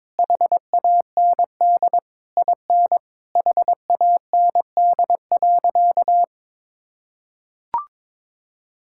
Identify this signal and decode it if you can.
Morse code, heard as a keyed tone, about 22 wpm, 700 Hz. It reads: HAND IN HAND.
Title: Sentences from Top 1000 Words - Review Part 5 - 22wpm